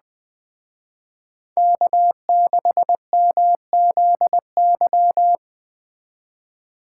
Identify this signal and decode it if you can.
Morse code, heard as a keyed tone, about 23 wpm, 700 Hz. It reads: K6MZY